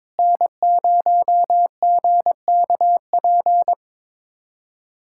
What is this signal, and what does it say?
Morse code, heard as a keyed tone, about 22 wpm, 700 Hz. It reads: N0GKP
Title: USA Callsign DE Callsign - 5 Characters - 22wpm